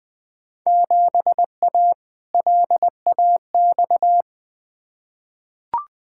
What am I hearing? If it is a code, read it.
Morse code, heard as a keyed tone, about 20 wpm, 700 Hz. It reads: 7A LAX
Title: Field Day Exchanges Encore - 20wpm